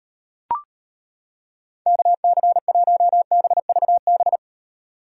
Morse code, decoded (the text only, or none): KC1BVB